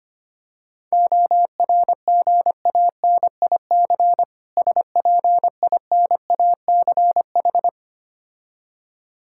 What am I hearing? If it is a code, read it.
Morse code, heard as a keyed tone, about 25 wpm, 690 Hz. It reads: ORGANIC SPINACH
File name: practice / 25wpm / Difficult Words with Hints - 25wpm